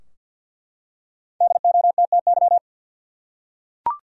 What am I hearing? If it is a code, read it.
Morse code, heard as a keyed tone, about 51 wpm, 700 Hz. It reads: DOTTY
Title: USA Common Names - 50wpm